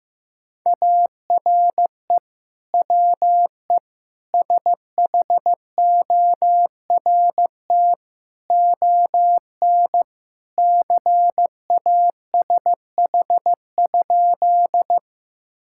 Morse code, heard as a keyed tone, about 15 wpm, 700 Hz. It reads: ARE WE SHORT ON CASH?
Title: Sentences from Top 1000 Words - Review Part 5 - 15wpm